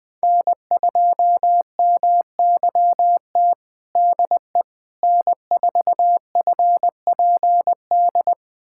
Morse code, heard as a keyed tone, about 21 wpm, 700 Hz. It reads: N2MYT DE N4FPD
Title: USA Callsign DE Callsign - 5 Characters - 20wpm